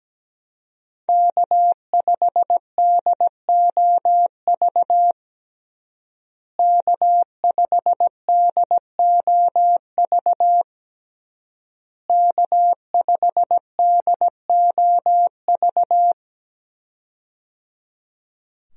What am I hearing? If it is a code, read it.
Morse code, heard as a keyed tone, about 17 wpm, 700 Hz. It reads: K5DOV K5DOV K5DOV
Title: USA Call Signs - 5 Characters - Repeated 3x - 17wpm